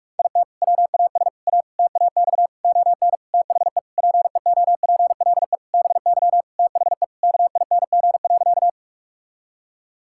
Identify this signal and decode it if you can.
Morse code, heard as a keyed tone, about 46 wpm, 700 Hz. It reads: IT WAS A TAX ON THE PEOPLE BY THE KING.
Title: Sentences from Top 1000 Words - Review Part 5 - 45wpm